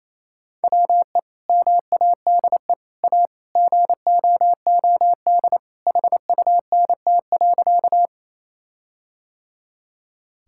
Morse code, heard as a keyed tone, about 27 wpm, 700 Hz. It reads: WE MADE A GOOD HUNT.
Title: Sentences from Top 1600 Words 28wpm